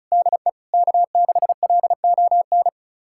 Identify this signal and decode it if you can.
Morse code, heard as a keyed tone, about 36 wpm, 700 Hz. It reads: DE K6LON
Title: USA Callsign DE Callsign - 5 Characters - 35wpm